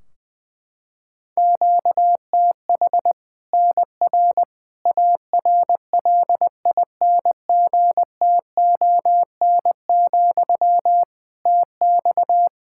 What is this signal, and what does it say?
Morse code, heard as a keyed tone, about 20 wpm, 700 Hz. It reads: QTH NR ARLINGTON, TX
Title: K1USN SST QSO Practice 20wpm